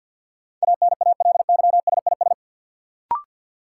Morse code, heard as a keyed tone, about 49 wpm, 700 Hz. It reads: ANALYSIS
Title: Sentences from Top 1000 Words - Review Part 5 - New Word Drill - 50wpm